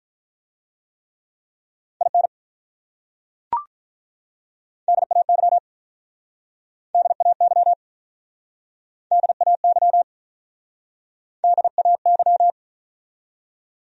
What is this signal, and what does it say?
Morse code, heard as a keyed tone, about 43 wpm, 700 Hz. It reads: IN DAY DAY DAY DAY
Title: Speed-Racing Top 100 Words 35wpm